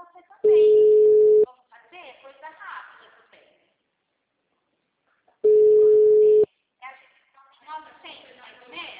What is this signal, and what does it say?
Ringback tone (Europe, 425 Hz), with speech.
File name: out- -2002- - - .31337